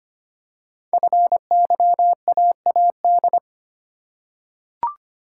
Morse code, heard as a keyed tone, about 25 wpm, 700 Hz. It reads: FYAAD
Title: Code Groups - 5 Letters - 25wpm